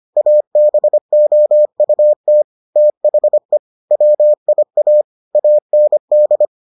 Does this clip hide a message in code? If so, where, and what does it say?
Yes: Morse code from 0.2 to 6.5 s, heard as a keyed tone, about 25 wpm, 600 Hz. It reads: ABOUT THE WIA AND